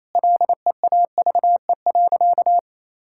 Morse code, heard as a keyed tone, about 28 wpm, 700 Hz. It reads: LEAVE.
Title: Sentences from Top 1000 Words - Review Part 5 - 28wpm